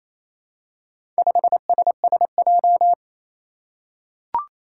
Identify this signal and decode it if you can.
Morse code, heard as a keyed tone, about 28 wpm, 700 Hz. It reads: 5SSJ
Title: Four Challenging Letter-Number Combinations 28wpm